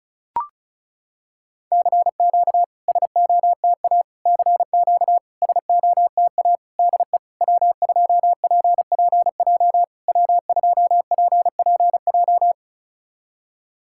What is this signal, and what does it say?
Morse code, heard as a keyed tone, about 34 wpm, 690 Hz. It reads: CQ SOTA CQ SOTA DE W2PPJ W2PPJ
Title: SOTA QSO Practice 35wpm